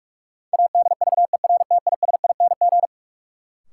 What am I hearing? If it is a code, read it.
Morse code, heard as a keyed tone, about 46 wpm, 700 Hz. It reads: ADVERTISING
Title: Sentences from Top 1000 Words - Review Part 6 - New Word Drill - 45wpm